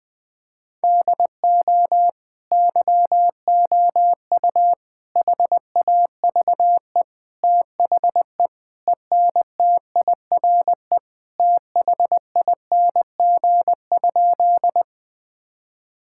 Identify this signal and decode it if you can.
Morse code, heard as a keyed tone, about 20 wpm, 700 Hz. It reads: DO YOU HAVE THE ENTIRE THING?
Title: Sentences from Top 1000 Words - Review Part 5 - 20wpm